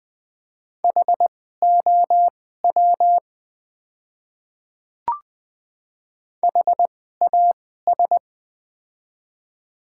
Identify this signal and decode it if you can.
Morse code, heard as a keyed tone, about 20 wpm, spaced as for 15 wpm, 700 Hz. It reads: HOW HAS
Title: 2x Character Spacing - Top 100 Words - 20wpm